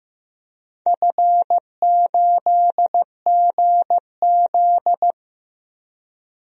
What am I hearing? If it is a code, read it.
Morse code, heard as a keyed tone, about 15 wpm, 700 Hz. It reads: F8GZ
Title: DXCC-100 Call Signs - 4 Characters - Repeated 2x - 15wpm